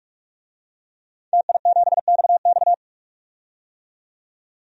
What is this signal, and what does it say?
Morse code, heard as a keyed tone, about 44 wpm, 700 Hz. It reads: TI7XX
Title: DXCC-100 Call Signs - 5 Characters - 45wpm